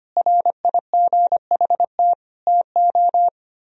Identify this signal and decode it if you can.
Morse code, heard as a keyed tone, about 31 wpm, 700 Hz. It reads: RIGHT TO